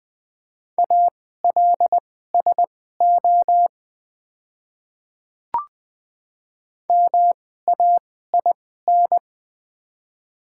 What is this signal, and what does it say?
Morse code, heard as a keyed tone, about 21 wpm, spaced as for 15 wpm, 700 Hz. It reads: ALSO MAIN
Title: 2x Character Spacing - Top 300 Words - 20wpm